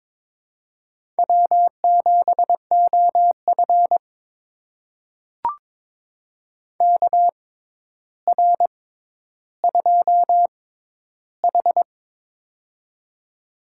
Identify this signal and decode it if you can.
Morse code, heard as a keyed tone, about 22 wpm, spaced as for 11 wpm, 700 Hz. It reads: W7OF KR2H
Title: USA Call Signs - 4 Characters - 6x Character Spacing - 22wpm